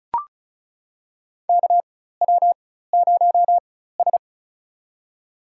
Morse code, heard as a keyed tone, about 36 wpm, spaced as for 16 wpm, 700 Hz. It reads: KW0S
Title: USA Call Signs - 4 Characters - 4x Character Spacing - 35wpm